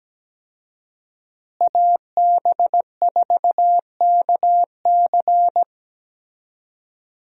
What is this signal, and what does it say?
Morse code, heard as a keyed tone, about 17 wpm, 700 Hz. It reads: AB4KC
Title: USA Call Signs - 5 Characters - Repeated 2x - 17wpm